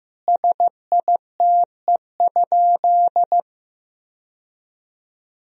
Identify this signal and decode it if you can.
Morse code, heard as a keyed tone, about 15 wpm, 700 Hz. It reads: SITE?
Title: Sentences from Top 1000 Words - Review Part 4 - 15wpm